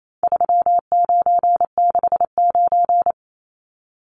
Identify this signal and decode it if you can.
Morse code, heard as a keyed tone, about 28 wpm, 700 Hz. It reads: 3969
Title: Numbers - 4-Digit 28wpm